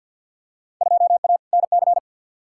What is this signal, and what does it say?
Morse code, heard as a keyed tone, about 53 wpm, 700 Hz. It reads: 2A NC
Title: Field Day Exchanges - 50wpm